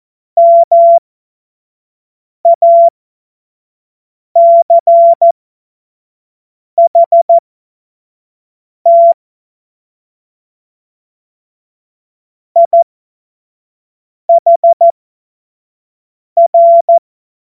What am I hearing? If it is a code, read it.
Morse code, heard as a keyed tone, about 14 wpm, spaced as for 5 wpm, 680 Hz. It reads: MACHT IHR
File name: Morsekurs / Witze / Witz 14. März 2023 25 F70